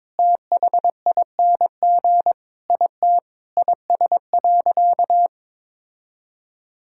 Morse code, heard as a keyed tone, about 22 wpm, 700 Hz. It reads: THING IT IS.
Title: Sentences from Top 1000 Words - Review Part 6 - 22wpm